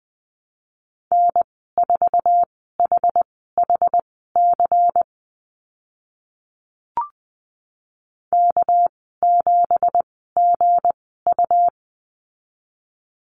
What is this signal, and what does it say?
Morse code, heard as a keyed tone, about 20 wpm, spaced as for 15 wpm, 700 Hz. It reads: N4HHC K7GU
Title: 200 Lesson 32g - Letter k - Call Signs 20wpm